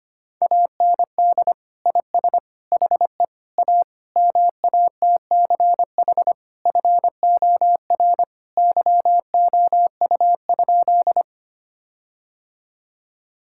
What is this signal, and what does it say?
Morse code, heard as a keyed tone, about 25 wpm, 700 Hz. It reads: AND IS HE A MATCH FOR YOU?
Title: Sentences from Top 1600 Words 25wpm